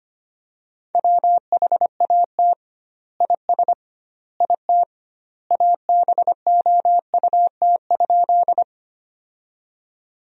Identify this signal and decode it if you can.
Morse code, heard as a keyed tone, about 25 wpm, 700 Hz. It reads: WHAT IS IT ABOUT?
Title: 2x Word-Spacing - Sentences from Top 100 Words - 25wpm